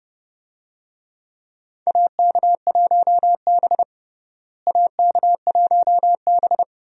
Morse code, heard as a keyed tone, about 30 wpm, 700 Hz. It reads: AK1B AK1B
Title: USA Call Signs - 4 Characters - Repeated 2x - 30wpm